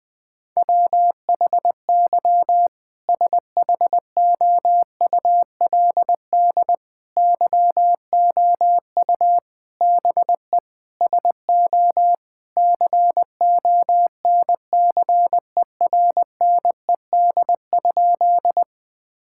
Morse code, heard as a keyed tone, about 20 wpm, 700 Hz. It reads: WHY SHOULD YOU BE SO CONCERNED?
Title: Sentences from Top 1000 Words - Review Part 6 - 20wpm